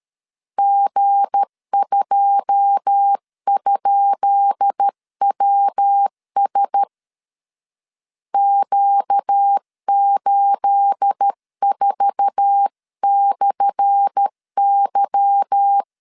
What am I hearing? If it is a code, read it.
Morse code, heard as a keyed tone, about 13 wpm, 800 Hz. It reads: G2?WS Q84/Y
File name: LEZIONI / 10 SIGNS / 10B INT 12wpm 2414